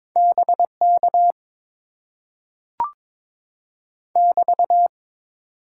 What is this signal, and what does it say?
Morse code, heard as a keyed tone, about 22 wpm, 700 Hz. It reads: BK <BT>
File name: Top 90 QSO Elements - 6x Character Spacing - 22wpm